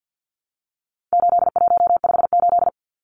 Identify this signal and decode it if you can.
Morse code, heard as a keyed tone, about 50 wpm, 700 Hz. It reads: 8158
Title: Numbers - 4-Digit 50wpm